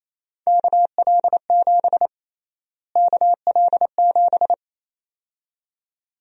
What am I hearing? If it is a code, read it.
Morse code, heard as a keyed tone, about 28 wpm, 700 Hz. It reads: KL7 KL7
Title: USA Call Sign Prefixes - Repeated 2x - 28wpm